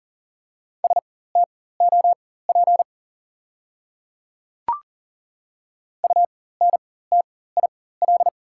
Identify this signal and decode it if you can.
Morse code, heard as a keyed tone, about 40 wpm, spaced as for 19 wpm, 690 Hz. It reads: STOP UNTIL
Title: 4x Character Spacing - Top 300 Words - 40wpm